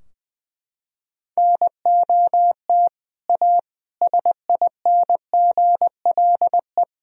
Morse code, heard as a keyed tone, about 21 wpm, 700 Hz. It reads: NOT A SINGLE
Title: Sentences from Top 1000 Words - Review Part 4 - 20wpm